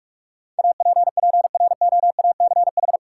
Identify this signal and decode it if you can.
Morse code, heard as a keyed tone, about 46 wpm, 700 Hz. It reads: APPROACH
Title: Sentences from Top 1000 Words - Review Part 6 - New Word Drill - 45wpm